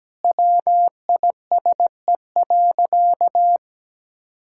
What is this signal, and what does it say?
Morse code, heard as a keyed tone, about 17 wpm, 700 Hz. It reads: WISE.
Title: Sentences from Top 1600 Words 17wpm